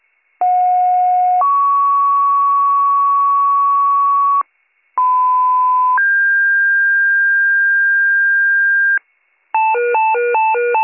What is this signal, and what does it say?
Two-tone page, heard as a one-second tone, then a three-second tone, about 730 then 1120 Hz, about 1010 then 1600 Hz. It is heard from 0.4 to 9.0 s.